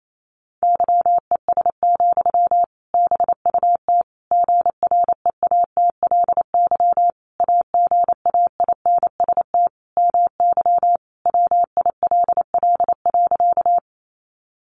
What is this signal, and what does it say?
Morse code, heard as a keyed tone, about 28 wpm, 700 Hz. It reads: YES, BUT GREATLY AGAINST MY WILL.
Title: Sentences from Top 1400 Words 28wpm